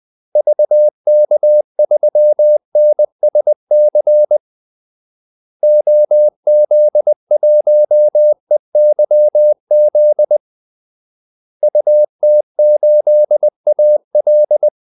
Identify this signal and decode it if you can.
Morse code, heard as a keyed tone, about 20 wpm, 600 Hz. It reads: VK3NSC OZ1EYZ UT8AL
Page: Kaldesignaler 20 wpm